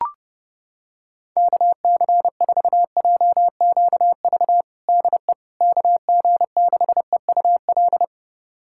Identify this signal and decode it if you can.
Morse code, heard as a keyed tone, about 32 wpm, 700 Hz. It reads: KC4JQV DE KG6EUL
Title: USA Callsign DE Callsign - 6 Characters - 30wpm